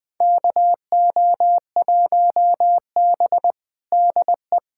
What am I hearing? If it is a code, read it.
Morse code, heard as a keyed tone, about 21 wpm, 700 Hz. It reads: KO1B DE
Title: USA Callsign DE Callsign - 4 Characters - 20wpm